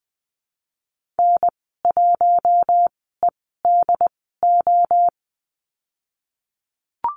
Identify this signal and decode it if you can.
Morse code, heard as a keyed tone, about 20 wpm, spaced as for 15 wpm, 700 Hz. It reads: N1EDO
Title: 173 Lesson 28g - Number 9 - Call Signs 20wpm